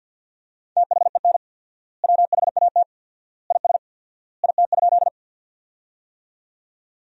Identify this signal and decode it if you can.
Morse code, heard as a keyed tone, about 50 wpm, 700 Hz. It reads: THEN WHAT IS IT?
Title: 4x Word-Spacing - Sentences from Top 100 Words - 50wpm